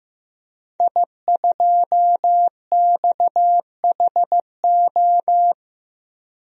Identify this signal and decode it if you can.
Morse code, heard as a keyed tone, about 15 wpm, 700 Hz. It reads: I2XHO
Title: DXCC-100 Call Signs - 5 Characters - Repeated 3x - 15wpm